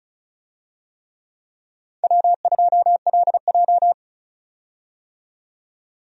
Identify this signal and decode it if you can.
Morse code, heard as a keyed tone, about 34 wpm, 700 Hz. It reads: W2LJ
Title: USA Call Signs - 4 Characters - 35wpm